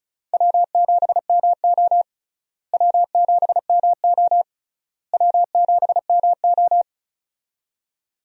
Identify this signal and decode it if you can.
Morse code, heard as a keyed tone, about 34 wpm, 700 Hz. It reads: W7MO W7MO W7MO
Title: USA Call Signs - 4 Characters - Repeated 3x - 35wpm